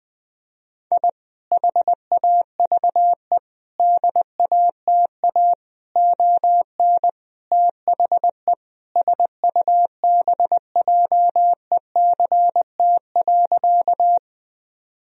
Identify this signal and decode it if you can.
Morse code, heard as a keyed tone, about 21 wpm, 700 Hz. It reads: I HAVE DATA ON THE SUBJECT.
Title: Sentences from Top 1000 Words - Review Part 3 - 20wpm